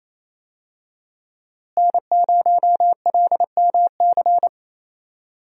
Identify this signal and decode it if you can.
Morse code, heard as a keyed tone, about 27 wpm, 700 Hz. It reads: N0LMC
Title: USA Call Signs - 5 Characters - Repeated 3x - 28wpm